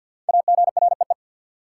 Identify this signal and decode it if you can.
Morse code, heard as a keyed tone, about 53 wpm, 700 Hz. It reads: AGREE
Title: Sentences from Top 1100 Words - New Word Drill Alt - 50wpm